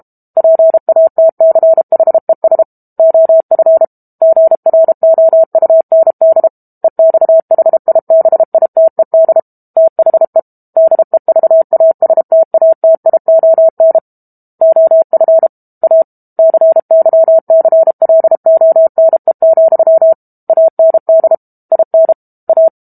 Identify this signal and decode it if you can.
Morse code, heard as a keyed tone, about 32 wpm, 650 Hz. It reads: PATCHES OF GROUND EXHIBITED THE DEVASTATION OF A CYCLONE, AND IN A
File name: Book Two / 06-Chapter 6 - 32 WPM